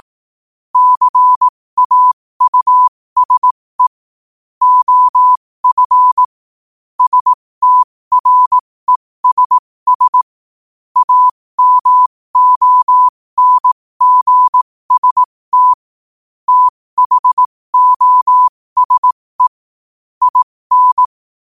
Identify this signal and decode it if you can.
Morse code, heard as a keyed tone, about 18 wpm, spaced as for 15 wpm, 1000 Hz. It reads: CAUSE OF STRESS AMONGST THOSE IN
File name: qotdfiles / 15wpm